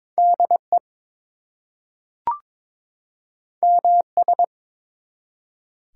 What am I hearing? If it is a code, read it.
Morse code, heard as a keyed tone, about 22 wpm, 700 Hz. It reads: DE MS
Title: US States and Canadian Provinces 22wpm